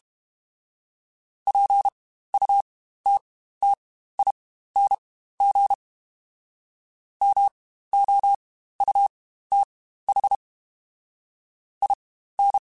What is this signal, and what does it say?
Morse code, heard as a keyed tone, about 32 wpm, spaced as for 15 wpm, 800 Hz. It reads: PUTTING MOUTH IN
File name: Spaced / Chuckles Spaced-mp3s / BrainMouth sp @30